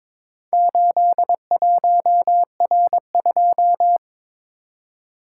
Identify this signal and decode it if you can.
Morse code, heard as a keyed tone, about 22 wpm, 700 Hz. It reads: 81R2
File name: practice / 22wpm / Four Letter-Number Combinations - 22wpm